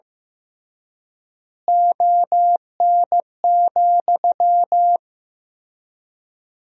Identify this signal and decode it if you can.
Morse code, heard as a keyed tone, about 15 wpm, 700 Hz. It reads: ON,
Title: 3x Word-Spacing - Sentences from Top 100 Words - 15wpm